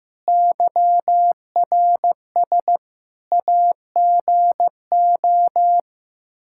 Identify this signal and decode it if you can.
Morse code, heard as a keyed tone, about 15 wpm, 700 Hz. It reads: YRS AGO
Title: Practice QSO - Exchange 3 - More Info - 15wpm